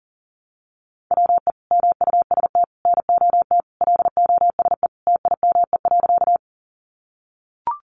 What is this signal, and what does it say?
Morse code, heard as a keyed tone, about 40 wpm, 700 Hz. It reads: WE MUST NOT LOSE TIME.
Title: Sentences from Top 1100 Words 40wpm